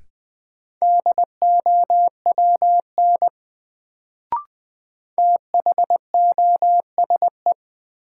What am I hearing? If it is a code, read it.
Morse code, heard as a keyed tone, about 20 wpm, 700 Hz. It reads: DOWN THOSE
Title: Sentences from Top 300 Words - New Word Drill - 20wpm